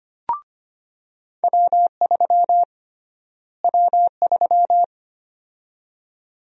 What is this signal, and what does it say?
Morse code, heard as a keyed tone, about 25 wpm, 700 Hz. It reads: W3 W3
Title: USA Call Sign Prefixes - Repeated 2x - 25wpm